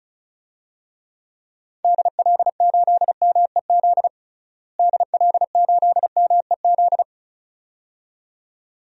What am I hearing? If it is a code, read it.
Morse code, heard as a keyed tone, about 34 wpm, 700 Hz. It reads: DL8MEZ DL8MEZ
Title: DXCC-100 Call Signs - 6 Characters - Repeated 2x - 35wpm